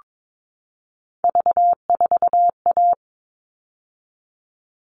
Morse code, heard as a keyed tone, about 22 wpm, 700 Hz. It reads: V4A
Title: All Cut Numbers - 3-Digit 22wpm